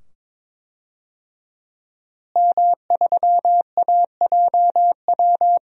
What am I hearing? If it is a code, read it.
Morse code, heard as a keyed tone, about 22 wpm, 700 Hz. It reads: M3AJW
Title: DXCC-100 Call Signs - 5 Characters - Repeated 2x - 22wpm